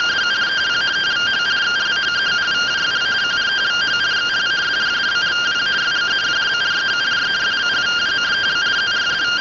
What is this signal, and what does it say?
Two-tone teleprinter signal, about 170 Hz shift.
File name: findings / SAGITAL nb203 fr61 3mm.gen-18 rtty rx